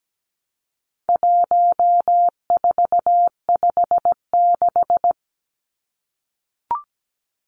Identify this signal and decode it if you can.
Morse code, heard as a keyed tone, about 17 wpm, 700 Hz. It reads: 1456
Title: Numbers - 4-Digit 17wpm